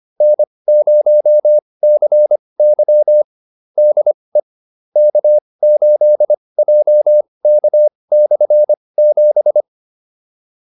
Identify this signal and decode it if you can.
Morse code, heard as a keyed tone, about 25 wpm, spaced as for 20 wpm, 600 Hz. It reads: N0CY DE K8JK/7
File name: QSO310a 20